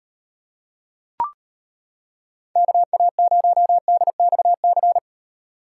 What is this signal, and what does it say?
Morse code, heard as a keyed tone, about 32 wpm, 700 Hz. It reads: KA0DXC